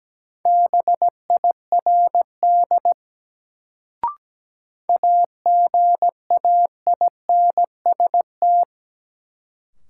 Morse code, heard as a keyed tone, about 17 wpm, 700 Hz. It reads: BIRD AGAINST
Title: Sentences from Top 400 Words - New Word Drill - 17wpm